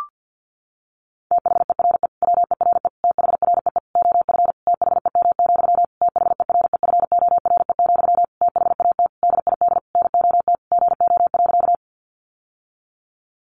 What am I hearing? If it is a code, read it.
Morse code, heard as a keyed tone, about 50 wpm, 700 Hz. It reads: THERE WERE THREE OF THEM, THEREFORE, THAT DID NOT GO.
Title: Sentences from Top 800 Words 50wpm